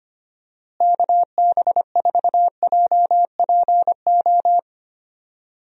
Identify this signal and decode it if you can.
Morse code, heard as a keyed tone, about 25 wpm, 700 Hz. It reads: KB4JPO
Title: USA Call Signs - 6 Characters - Repeated 3x - 25wpm